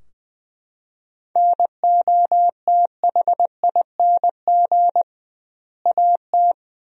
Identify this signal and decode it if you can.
Morse code, heard as a keyed tone, about 20 wpm, 700 Hz. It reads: NOTHING AT
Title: 2x Word-Spacing - Sentences from Top 500 Words - 20wpm